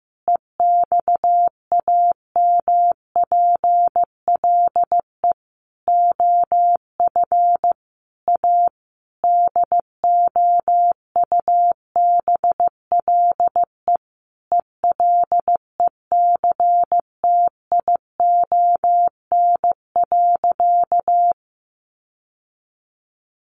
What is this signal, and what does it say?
Morse code, heard as a keyed tone, about 15 wpm, 700 Hz. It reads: EXAMPLE OF A DOUBLE ELECTION.
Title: Sentences from Top 1500 Words 15wpm